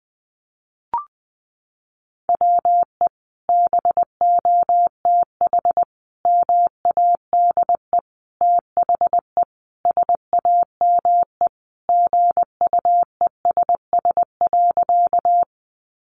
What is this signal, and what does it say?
Morse code, heard as a keyed tone, about 20 wpm, 700 Hz. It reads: WE BOTH MADE THE SAME GUESS.
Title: Sentences from Top 1500 Words 20wpm